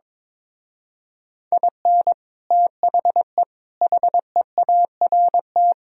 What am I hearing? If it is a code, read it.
Morse code, heard as a keyed tone, about 22 wpm, 700 Hz. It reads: IN THE HEART
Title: Speed-Racing - Sets of 5 Words 22wpm